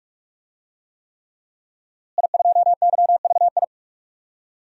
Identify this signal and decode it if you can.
Morse code, heard as a keyed tone, about 43 wpm, 700 Hz. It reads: I2YVI
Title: DXCC-100 Call Signs - 5 Characters - Repeated 3x - 45wpm